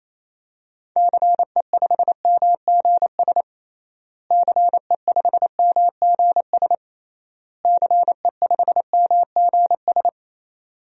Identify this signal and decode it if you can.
Morse code, heard as a keyed tone, about 28 wpm, 700 Hz. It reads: CE5MGS CE5MGS CE5MGS
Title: DXCC-100 Call Signs - 6 Characters - Repeated 3x - 28wpm